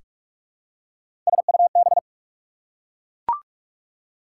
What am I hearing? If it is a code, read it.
Morse code, heard as a keyed tone, about 44 wpm, 690 Hz. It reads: SUB
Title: Prefixes and Suffixes 45wpm